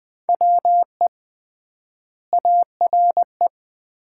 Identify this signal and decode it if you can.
Morse code, heard as a keyed tone, about 21 wpm, 700 Hz. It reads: WE ARE
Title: 3x Word-Spacing - Sets of 2 Words - 20wpm